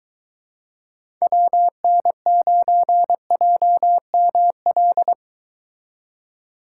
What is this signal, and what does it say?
Morse code, heard as a keyed tone, about 23 wpm, 700 Hz. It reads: WN9JML